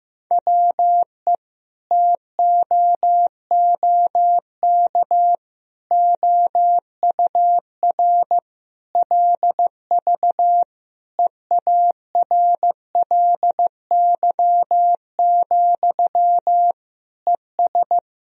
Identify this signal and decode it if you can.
Morse code, heard as a keyed tone, about 15 wpm, 700 Hz. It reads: WE TOOK OUR LV EARLY, ES
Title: Sentences from Top 400 Words and Abbreviations 15wpm